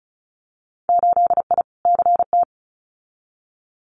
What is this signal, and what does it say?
Morse code, heard as a keyed tone, about 36 wpm, 700 Hz. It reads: 8I CT
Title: Winter Field Day Exchanges Encore 35wpm